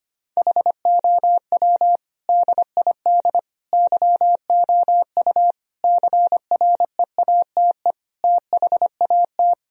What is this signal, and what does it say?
Morse code, heard as a keyed tone, about 25 wpm, 700 Hz. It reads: HOW DID YOU CREATE THAT
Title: Sentences from Top 1000 Words - Review Part 3 - 25wpm